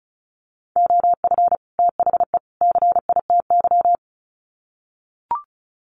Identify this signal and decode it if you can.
Morse code, heard as a keyed tone, about 34 wpm, 700 Hz. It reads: OF THE CITY
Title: Sets of 3 Words 35wpm